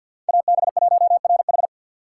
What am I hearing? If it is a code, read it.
Morse code, heard as a keyed tone, about 52 wpm, 700 Hz. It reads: AB1RH
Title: USA Call Signs - 5 Characters - 3x Character Spacing - 50wpm